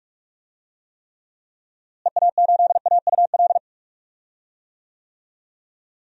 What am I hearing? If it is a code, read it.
Morse code, heard as a keyed tone, about 46 wpm, 700 Hz. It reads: EA8AUL
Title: DXCC-100 Call Signs - 6 Characters - Repeated 3x - 45wpm